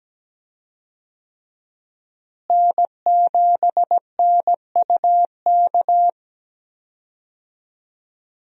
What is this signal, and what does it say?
Morse code, heard as a keyed tone, about 17 wpm, 700 Hz. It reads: N7NUK
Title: USA Call Signs - 5 Characters - Repeated 3x - 17wpm